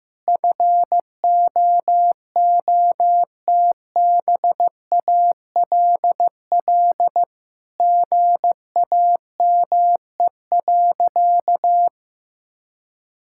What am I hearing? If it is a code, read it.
Morse code, heard as a keyed tone, about 15 wpm, 700 Hz. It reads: FOOTBALL GAME.
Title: Sentences from Top 1600 Words 15wpm